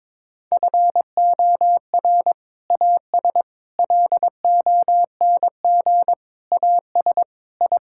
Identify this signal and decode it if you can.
Morse code, heard as a keyed tone, about 22 wpm, 700 Hz. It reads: FOR AS LONG AS I
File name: practice / 22wpm / Sets of 8 Words - 22wpm